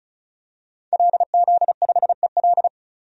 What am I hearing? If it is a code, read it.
Morse code, heard as a keyed tone, about 36 wpm, 700 Hz. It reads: LZ5EL
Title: DXCC-100 Call Signs - 5 Characters - 35wpm